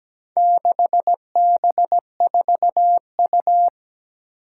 Morse code, heard as a keyed tone, about 17 wpm, 700 Hz. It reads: 6B4U
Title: Four Challenging Letter-Number Combinations 17wpm